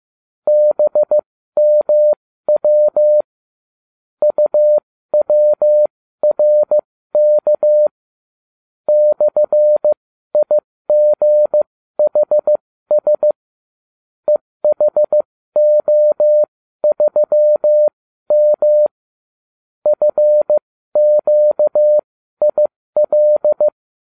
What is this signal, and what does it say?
Morse code, heard as a keyed tone, about 15 wpm, spaced as for 13 wpm, 600 Hz. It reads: BMW UWRK /IGHS EHO3M FQIL